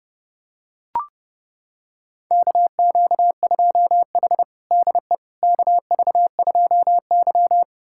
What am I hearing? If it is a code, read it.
Morse code, heard as a keyed tone, about 32 wpm, 700 Hz. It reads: KQ2H DE KV2Y
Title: USA Callsign DE Callsign - 4 Characters - 30wpm